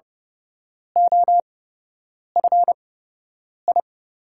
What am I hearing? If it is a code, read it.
Morse code, heard as a keyed tone, about 32 wpm, spaced as for 8 wpm, 700 Hz. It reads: OFI
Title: USA Call Signs - 8x Character Spacing - 30wpm